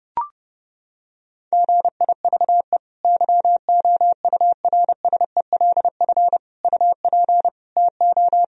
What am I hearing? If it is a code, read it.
Morse code, heard as a keyed tone, about 30 wpm, 700 Hz. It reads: GIVE YOURSELF UP TO
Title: Sentences from Top 1000 Words - Review Part 4 - 30wpm